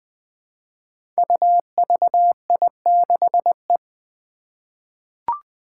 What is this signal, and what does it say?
Morse code, heard as a keyed tone, about 20 wpm, 700 Hz. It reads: UVI6E
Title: Code Groups - 5 Letter:Numbers - 20wpm